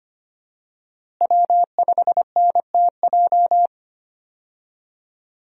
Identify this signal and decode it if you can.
Morse code, heard as a keyed tone, about 25 wpm, 700 Hz. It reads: W5NTJ
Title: USA Call Signs - 5 Characters - Repeated 3x - 25wpm